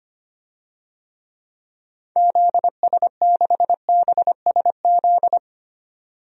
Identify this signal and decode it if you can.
Morse code, heard as a keyed tone, about 25 wpm, 700 Hz. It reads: ZS6BSZ
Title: DXCC-100 Call Signs - 6 Characters - 25wpm